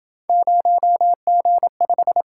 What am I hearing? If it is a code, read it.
Morse code, heard as a keyed tone, about 28 wpm, 700 Hz. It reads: 0G5